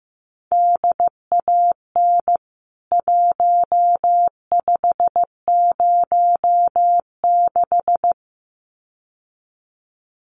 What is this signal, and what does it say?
Morse code, heard as a keyed tone, about 15 wpm, 700 Hz. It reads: DAN 1506
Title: CWT Exchange 15wpm